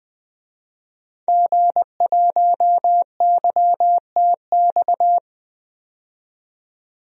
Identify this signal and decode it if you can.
Morse code, heard as a keyed tone, about 20 wpm, 700 Hz. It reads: G1YTX
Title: DXCC-100 Call Signs - 5 Characters - 20wpm